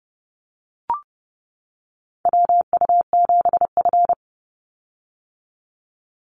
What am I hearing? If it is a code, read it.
Morse code, heard as a keyed tone, about 30 wpm, 700 Hz. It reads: WU7F
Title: Call Signs - 30wpm